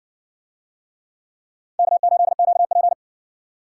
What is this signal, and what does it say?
Morse code, heard as a keyed tone, about 60 wpm, 700 Hz. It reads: K8QP